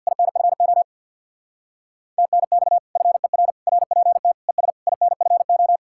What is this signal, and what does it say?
Morse code, heard as a keyed tone, about 64 wpm, 700 Hz. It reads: INFO TNX FER RPT ES INFO